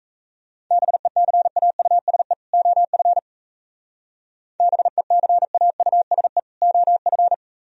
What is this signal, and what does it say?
Morse code, heard as a keyed tone, about 32 wpm, 690 Hz. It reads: BECAUSE OF BECAUSE OF
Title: Speed-Racing - Sets of 2 Words 28wpm